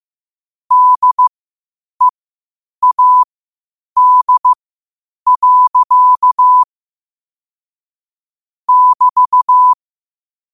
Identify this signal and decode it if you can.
Morse code, heard as a keyed tone, about 15 wpm, spaced as for 8 wpm, 1000 Hz. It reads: DEAD. <BT>
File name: qotdfiles / 8wpm